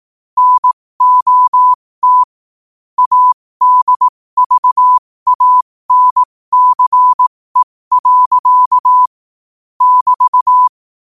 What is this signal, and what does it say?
Morse code, heard as a keyed tone, about 18 wpm, spaced as for 15 wpm, 1000 Hz. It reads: NOT ADVANCE. <BT>
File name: qotdfiles / 15wpm